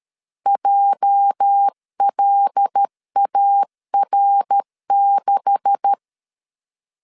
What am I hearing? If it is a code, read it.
Morse code, heard as a keyed tone, about 13 wpm, 800 Hz. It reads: JLAR6
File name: LEZIONI / 9 286 12wpm 2305